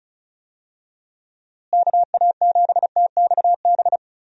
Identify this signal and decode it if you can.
Morse code, heard as a keyed tone, about 36 wpm, 700 Hz. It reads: KA7TXB
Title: USA Call Signs - 6 Characters - Repeated 2x - 35wpm